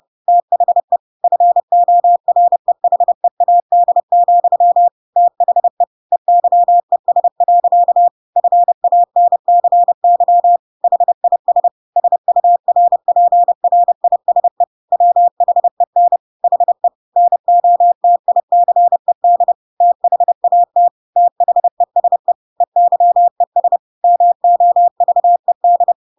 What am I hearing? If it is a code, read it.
Morse code, heard as a keyed tone, about 30 wpm, 700 Hz. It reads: THE FOREHEAD, THE EYES. FANCY HIS SURPRISE WHEN HE NOTICED THAT THESE EYES MOVED
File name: Pinocchio / Pinocchio ch3 30 30 ik1hge